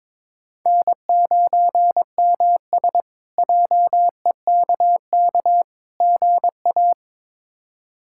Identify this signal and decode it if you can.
Morse code, heard as a keyed tone, about 22 wpm, 700 Hz. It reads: N9MS JEKK GA
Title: K1USN SST QSO Practice - 22wpm